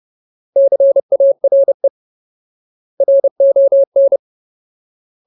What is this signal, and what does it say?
Morse code, heard as a keyed tone, about 30 wpm, 550 Hz. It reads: CARE RON
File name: Ultimate Word List / 30WPM Ultimate